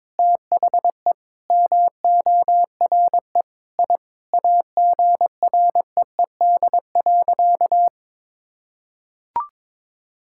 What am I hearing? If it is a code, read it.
Morse code, heard as a keyed tone, about 22 wpm, 700 Hz. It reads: THE MORE I AGREED.
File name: practice / 22wpm / Sentences with 8-Word Phrases - 22wpm